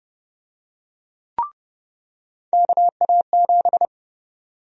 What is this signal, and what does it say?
Morse code, heard as a keyed tone, about 30 wpm, 700 Hz. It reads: KA7
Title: USA Call Sign Prefixes - 30wpm